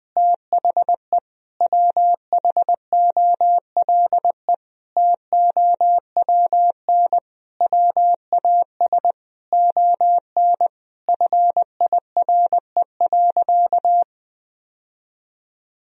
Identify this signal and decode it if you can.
Morse code, heard as a keyed tone, about 21 wpm, 700 Hz. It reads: THE WHOLE TOWN WAS ON FIRE.
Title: Sentences from Top 1000 Words - Review Part 2 - 20wpm